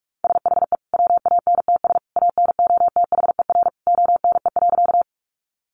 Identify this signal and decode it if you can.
Morse code, heard as a keyed tone, about 44 wpm, 700 Hz. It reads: SHE WANTS ANOTHER ONE.
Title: Easy Phrases 45wpm